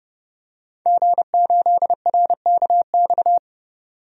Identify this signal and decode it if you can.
Morse code, heard as a keyed tone, about 31 wpm, 700 Hz. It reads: G8RKX
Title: DXCC-100 Call Signs - 5 Characters - 30wpm